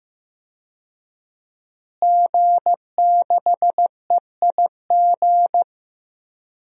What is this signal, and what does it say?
Morse code, heard as a keyed tone, about 15 wpm, 700 Hz. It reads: G6EIG
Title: DXCC-100 Call Signs - 5 Characters - Repeated 3x - 15wpm